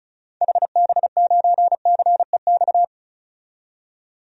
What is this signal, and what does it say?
Morse code, heard as a keyed tone, about 36 wpm, 700 Hz. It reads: HB9CEX
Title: DXCC-100 Call Signs - 6 Characters - 35wpm